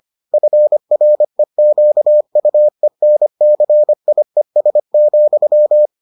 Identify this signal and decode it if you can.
Morse code, heard as a keyed tone, about 25 wpm, 600 Hz. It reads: FREQUENCIES,